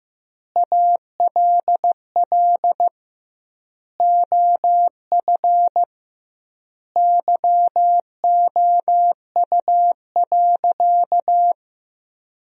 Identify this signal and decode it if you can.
Morse code, heard as a keyed tone, about 15 wpm, 700 Hz. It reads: ALL OF YOU.
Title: 2x Word-Spacing - Sentences from Top 500 Words - 15wpm